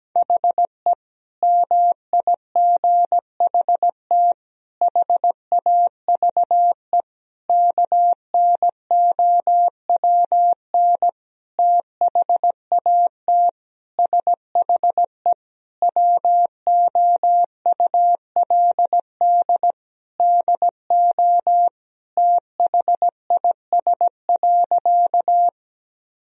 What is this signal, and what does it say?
Morse code, heard as a keyed tone, about 17 wpm, 700 Hz. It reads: HE MIGHT HAVE KNOWN THAT SHE WOULD DO THIS.
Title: Sentences from Top 1000 Words - Review Part 3 - 17wpm